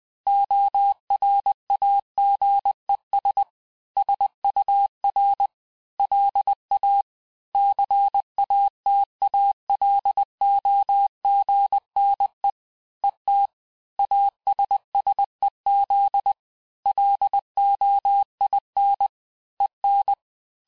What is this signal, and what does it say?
Morse code, heard as a keyed tone, about 20 wpm, 800 Hz. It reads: ORAGES SUR LA CATALOGNE ET ASSEZ LOIN EN
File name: cours-du-lundi files / cours-1584-Fin-de-QSO-vitesse-19mpm